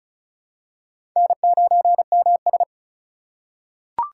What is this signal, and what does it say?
Morse code, heard as a keyed tone, about 36 wpm, 700 Hz. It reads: N9MS
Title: K1USN SST QSO Practice - 35wpm